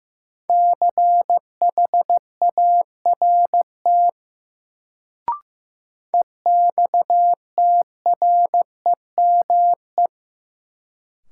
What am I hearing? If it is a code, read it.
Morse code, heard as a keyed tone, about 15 wpm, 700 Hz. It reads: CHART EXTREME
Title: Sentences from Top 1300 Words - New Word Drill - 15wpm